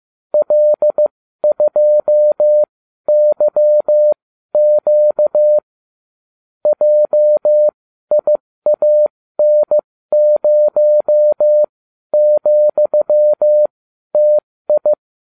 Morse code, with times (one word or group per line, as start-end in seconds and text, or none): 0.3-5.6 L2YQ
6.7-14.9 JIAN0,TI